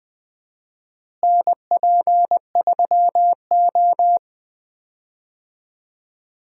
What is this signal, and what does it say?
Morse code, heard as a keyed tone, about 21 wpm, 700 Hz. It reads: NP3O
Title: USA Call Signs - 4 Characters - Repeated 2x - 20wpm